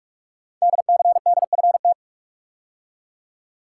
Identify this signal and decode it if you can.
Morse code, heard as a keyed tone, about 44 wpm, 700 Hz. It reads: DCDFT
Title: Code Groups - 5 Letters - 45wpm